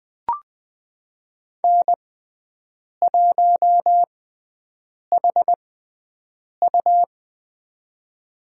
Morse code, heard as a keyed tone, about 20 wpm, spaced as for 7 wpm, 700 Hz. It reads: N1HU
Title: USA Call Signs - 4 Characters - 6x Character Spacing - 20wpm